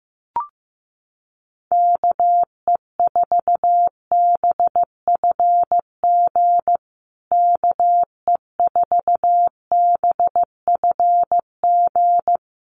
Morse code, heard as a keyed tone, about 15 wpm, 700 Hz. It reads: KE4BFG KE4BFG
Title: USA Call Signs Repeated 3x 15wpm